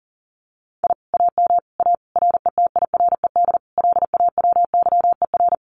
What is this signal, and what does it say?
Morse code, heard as a keyed tone, about 40 wpm, 700 Hz. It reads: I AM A RETIRED LAWYER
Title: Common QSO Phrases Encore 40wpm